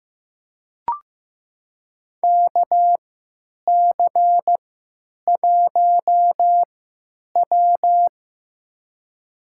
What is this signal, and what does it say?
Morse code, heard as a keyed tone, about 15 wpm, spaced as for 8 wpm, 700 Hz. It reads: KC1W
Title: USA Call Signs - 4 Characters - 3x Character Spacing - 15wpm